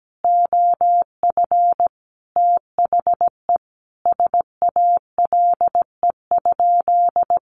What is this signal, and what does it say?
Morse code, heard as a keyed tone, about 17 wpm, 700 Hz. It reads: OF THE SALE?
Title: Sentences from Top 1400 Words 17wpm